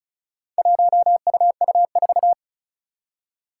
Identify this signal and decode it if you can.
Morse code, heard as a keyed tone, about 36 wpm, 700 Hz. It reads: 1UU4
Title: Four Challenging Letter-Number Combinations 35wpm